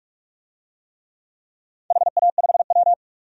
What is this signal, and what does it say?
Morse code, heard as a keyed tone, about 47 wpm, 700 Hz. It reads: HA5W
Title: DXCC-100 Call Signs - 4 Characters - 45wpm